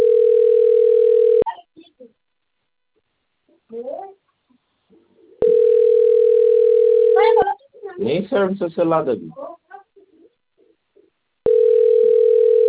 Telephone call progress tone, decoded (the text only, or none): ringback tone